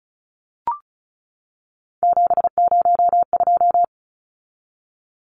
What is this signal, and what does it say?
Morse code, heard as a keyed tone, about 34 wpm, 700 Hz. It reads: 702
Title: Numbers - 3-Digit 35wpm